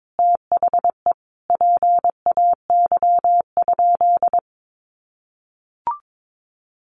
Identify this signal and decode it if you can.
Morse code, heard as a keyed tone, about 22 wpm, 700 Hz. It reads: THE PAY?
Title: Sentences from Top 700 Words 22wpm